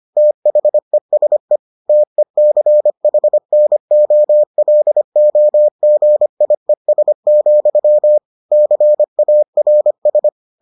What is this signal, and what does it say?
Morse code, heard as a keyed tone, about 25 wpm, 600 Hz. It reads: THESE TECHNOLOGIES, CARS